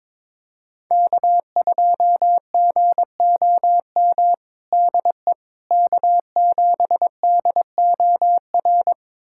Morse code, heard as a keyed tone, about 22 wpm, 700 Hz. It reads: K2GOM DE K7DOR
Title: USA Callsign DE Callsign - 5 Characters - 22wpm